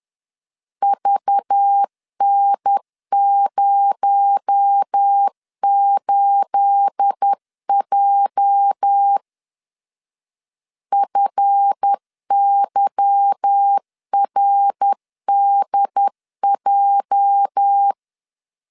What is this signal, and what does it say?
Morse code, heard as a keyed tone, about 10 wpm, 800 Hz. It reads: VN08J FYRDJ
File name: CW / DET / 1 9 1500C S B 10wpm 4439